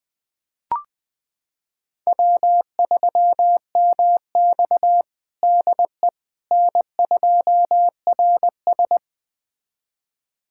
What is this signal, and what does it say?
Morse code, heard as a keyed tone, about 20 wpm, 700 Hz. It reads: W3MX DE N2RS
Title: USA Callsign DE Callsign - 4 Characters - 20wpm